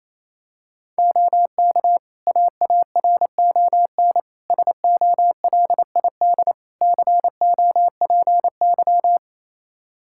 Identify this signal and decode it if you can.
Morse code, heard as a keyed tone, about 27 wpm, 700 Hz. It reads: OK AARON SOLID COPY
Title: K1USN SST QSO Practice 28wpm